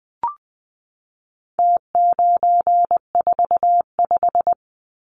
Morse code, heard as a keyed tone, about 20 wpm, 700 Hz. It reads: T945
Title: Numbers - Contest Style - 4-Digit 20wpm